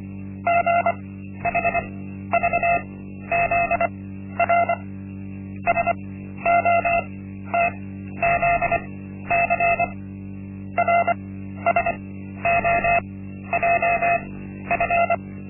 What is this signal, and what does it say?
Morse code, heard as a keyed tone, about 24 wpm, spaced as for 12 wpm, 690 Hz. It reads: GHVZR SOTZC RSOJF
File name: Linksmakalnis-1